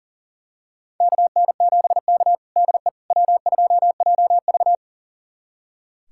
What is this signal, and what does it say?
Morse code, heard as a keyed tone, about 40 wpm, 700 Hz. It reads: KN7K DE W2JV
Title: USA Callsign DE Callsign - 4 Characters - 40wpm